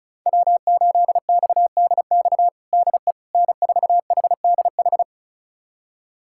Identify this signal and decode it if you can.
Morse code, heard as a keyed tone, about 36 wpm, 700 Hz. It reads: W8XDX DE N4HDH
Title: USA Callsign DE Callsign - 5 Characters - 35wpm